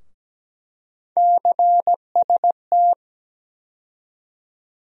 Morse code, heard as a keyed tone, about 17 wpm, 700 Hz. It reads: CST
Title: USA Call Sign Postfixes - 3 Letters 17wpm